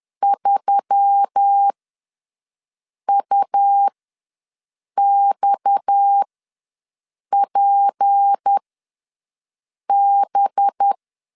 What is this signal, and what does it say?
Morse code, heard as a keyed tone, about 10 wpm, spaced as for 5 wpm, 800 Hz. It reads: 3UXPB
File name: LEZIONI / 6 PKX0 / 6 PKX0 ws 10wpm 4220